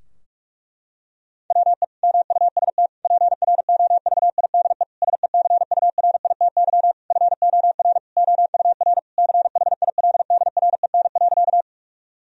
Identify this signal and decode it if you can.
Morse code, heard as a keyed tone, about 43 wpm, 700 Hz. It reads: WE MUST PROVIDE SECURITY FOR OUR CHILDREN.
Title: Sentences from Top 1000 Words - Review Part 4 - 45wpm